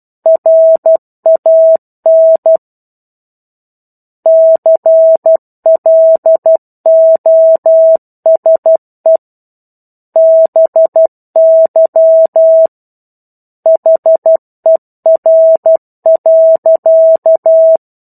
Morse code, heard as a keyed tone, about 12 wpm, 650 Hz. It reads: RAN CLOSE BY HER.
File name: 01-Chapter-01 12 WPM